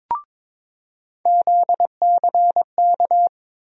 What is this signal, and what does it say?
Morse code, heard as a keyed tone, about 23 wpm, 700 Hz. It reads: ZCK
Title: USA Call Sign Postfixes - 3 Letters 22wpm